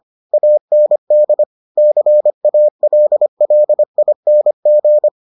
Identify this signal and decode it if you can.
Morse code, heard as a keyed tone, about 25 wpm, 600 Hz. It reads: AND CALLING